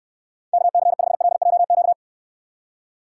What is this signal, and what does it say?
Morse code, heard as a keyed tone, about 69 wpm, 700 Hz. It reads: KC5LJQ